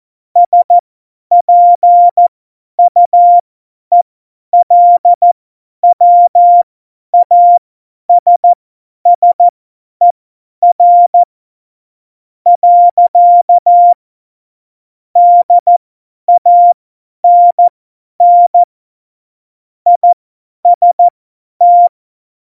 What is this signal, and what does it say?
Morse code, heard as a keyed tone, about 14 wpm, spaced as for 10 wpm, 700 Hz. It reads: SPUELWASSER. DANN IST
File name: Morsekurs / Witze / Witz 18. Juni 2019, 50 F70